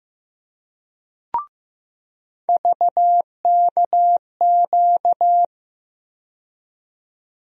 Morse code, heard as a keyed tone, about 15 wpm, 700 Hz. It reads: VKQ
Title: USA Call Sign Postfixes - 3 Letters 15wpm